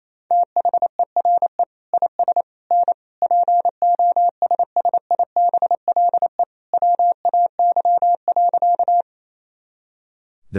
Morse code, heard as a keyed tone, about 28 wpm, 700 Hz. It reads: THERE IS N POSSIBLE WAY.
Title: Sentences from Top 500 Words and Abbreviations 28wpm